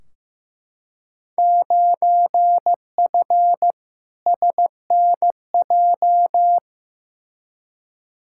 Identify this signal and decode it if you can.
Morse code, heard as a keyed tone, about 15 wpm, 700 Hz. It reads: 9F SNJ
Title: Field Day Exchanges Encore - 15wpm